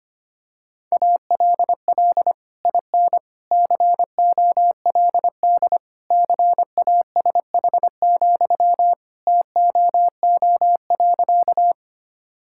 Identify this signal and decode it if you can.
Morse code, heard as a keyed tone, about 25 wpm, 700 Hz. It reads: ALL IN COLD CASH, TOO.
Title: Sentences from Top 1000 Words - Review Part 5 - 25wpm